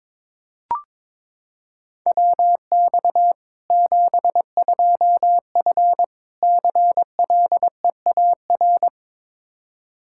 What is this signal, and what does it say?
Morse code, heard as a keyed tone, about 22 wpm, 700 Hz. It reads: WX 72F CLEAR
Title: Practice QSO - Exchange 2 - Info - 22wpm